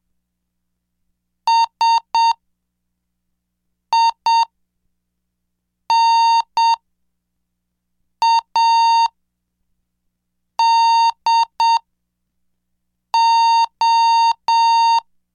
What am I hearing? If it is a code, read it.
Morse code, heard as a keyed tone, about 7 wpm, spaced as for 4 wpm, 920 Hz. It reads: SINADO
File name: cursos / CDestudo / 11 Track 11